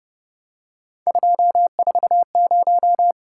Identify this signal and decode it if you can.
Morse code, heard as a keyed tone, about 29 wpm, 700 Hz. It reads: 240